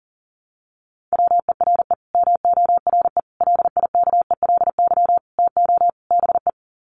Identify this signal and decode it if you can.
Morse code, heard as a keyed tone, about 40 wpm, 700 Hz. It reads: WERE MORE LIKELY TO BE
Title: Sets of 5 Words Encore2 40wpm